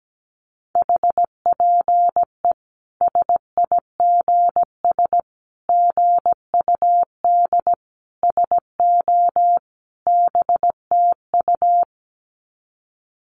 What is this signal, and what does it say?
Morse code, heard as a keyed tone, about 17 wpm, 700 Hz. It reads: HPE SIGS GUD SO BTU
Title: Common QSO Phrases 17wpm